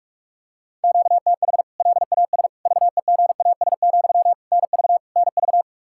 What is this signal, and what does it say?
Morse code, heard as a keyed tone, about 46 wpm, 700 Hz. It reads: QTH LAS VEGAS, NV NV
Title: K1USN SST QSO Practice 45wpm